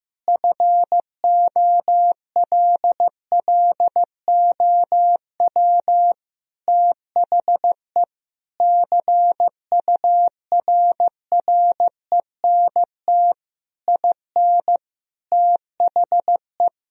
Morse code, heard as a keyed tone, about 14 wpm, 700 Hz. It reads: FOLLOW THE CURRENT IN THE
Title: Sentences from Top 1000 Words - Review Part 4 - 15wpm